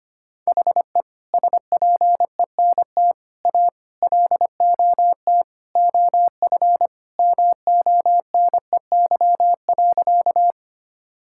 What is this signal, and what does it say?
Morse code, heard as a keyed tone, about 25 wpm, 700 Hz. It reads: HE SPENT A LOT OF MONEY.
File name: practice / 25wpm / Sentences with 4-Word Phrases - 25wpm